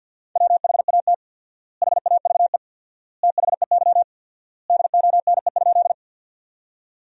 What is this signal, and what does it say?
Morse code, heard as a keyed tone, about 50 wpm, 700 Hz. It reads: WHAT HAVE THEY DONE?
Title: 4x Word-Spacing - Sentences from Top 500 Words - 50wpm